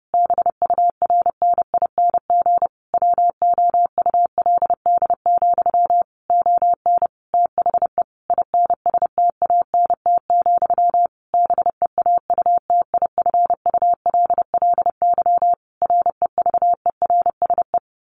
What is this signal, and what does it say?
Morse code, heard as a keyed tone, about 30 wpm, 700 Hz. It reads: BURNING WOULD, ON THE INSTANT, BEAUTIFULLY REVERSE
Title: A Sound Of Thunder 30wpm